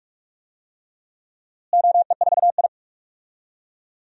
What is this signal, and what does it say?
Morse code, heard as a keyed tone, about 44 wpm, 700 Hz. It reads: OE4I
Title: DXCC-100 Call Signs - 4 Characters - Repeated 3x - 45wpm